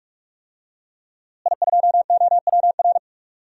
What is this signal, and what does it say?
Morse code, heard as a keyed tone, about 42 wpm, 700 Hz. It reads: I2OWR
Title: DXCC-100 Call Signs - 5 Characters - Repeated 2x - 45wpm